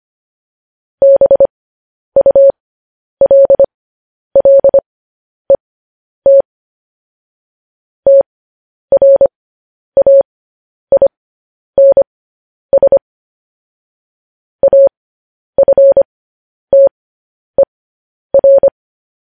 Morse code, heard as a keyed tone, about 25 wpm, spaced as for 10 wpm, 550 Hz. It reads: BULLET TRAINS AFTER
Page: news-headlines-in-morse-code-at-10-wpm